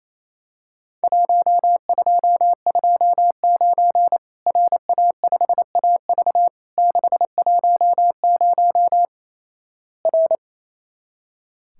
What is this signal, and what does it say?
Morse code, heard as a keyed tone, about 28 wpm, 700 Hz. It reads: 1229 RA5AV 610 R
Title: Worked All Europe DX Contest - QTC Exchange - 28wpm